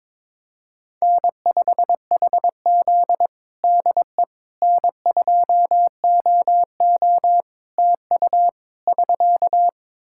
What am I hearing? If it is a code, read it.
Morse code, heard as a keyed tone, about 22 wpm, 700 Hz. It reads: N5HZ DE N2OO TU <SK>
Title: Practice QSO - Exchange 4 - Ending - 22wpm